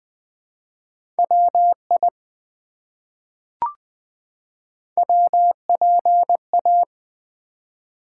Morse code, heard as a keyed tone, about 20 wpm, 700 Hz. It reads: WI WPA
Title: ARRL Sections v2 - 20wpm